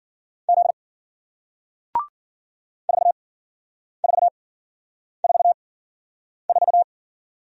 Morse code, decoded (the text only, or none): Z 4 4 4 4